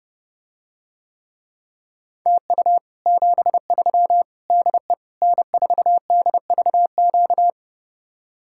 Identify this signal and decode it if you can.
Morse code, heard as a keyed tone, about 32 wpm, 700 Hz. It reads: TU 73 DE N4DVQ